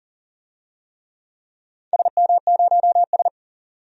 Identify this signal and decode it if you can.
Morse code, heard as a keyed tone, about 40 wpm, 700 Hz. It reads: SM0S
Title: DXCC-100 Call Signs - 4 Characters - 40wpm